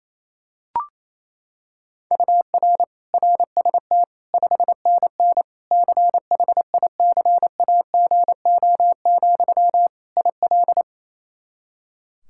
Morse code, heard as a keyed tone, about 28 wpm, 700 Hz. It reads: UR RST 5NN CHICAGO, IL
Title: K1USN SST QSO Practice 28wpm